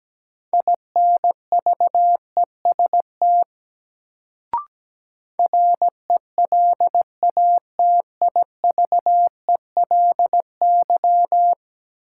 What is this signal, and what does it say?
Morse code, heard as a keyed tone, about 17 wpm, 700 Hz. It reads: INVEST RELATIVELY
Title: Sentences from Top 1100 Words - New Word Drill - 17wpm